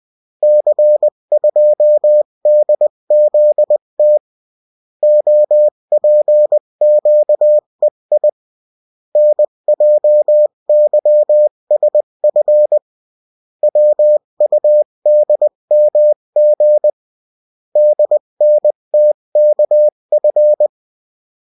Morse code, heard as a keyed tone, about 20 wpm, 600 Hz. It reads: C2DZT OPQEI NJYSF WUDMG DNTKF
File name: morse mp3 / revision9-vitesse 18 mots